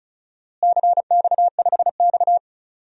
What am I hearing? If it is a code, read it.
Morse code, heard as a keyed tone, about 36 wpm, 700 Hz. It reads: CX5X
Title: DXCC-100 Call Signs - 4 Characters - Repeated 2x - 35wpm